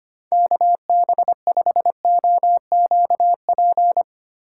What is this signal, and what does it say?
Morse code, heard as a keyed tone, about 25 wpm, 700 Hz. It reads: KB5OQP
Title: USA Call Signs - 6 Characters - 2x Character Spacing - 25wpm